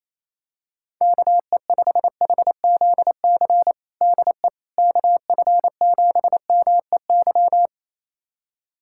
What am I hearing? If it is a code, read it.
Morse code, heard as a keyed tone, about 28 wpm, 700 Hz. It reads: KE5HZC DE KF7MEY
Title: USA Callsign DE Callsign - 6 Characters - 28wpm